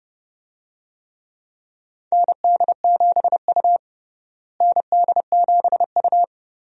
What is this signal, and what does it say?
Morse code, heard as a keyed tone, about 32 wpm, 700 Hz. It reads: ND7U ND7U
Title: USA Call Signs - 4 Characters - Repeated 3x - 30wpm